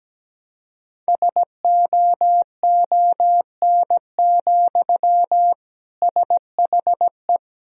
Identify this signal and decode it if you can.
Morse code, heard as a keyed tone, about 17 wpm, 700 Hz. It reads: SOON, SHE
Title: Sentences from Top 1000 Words - Review Part 1 - 17wpm